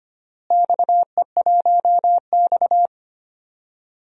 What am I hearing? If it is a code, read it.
Morse code, heard as a keyed tone, about 25 wpm, 700 Hz. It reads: XE1X
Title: DXCC-100 Call Signs - 4 Characters - Repeated 3x - 25wpm